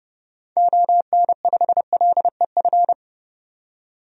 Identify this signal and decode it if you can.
Morse code, heard as a keyed tone, about 30 wpm, 700 Hz. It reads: ON5LEF
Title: DXCC-100 Call Signs - 6 Characters - 30wpm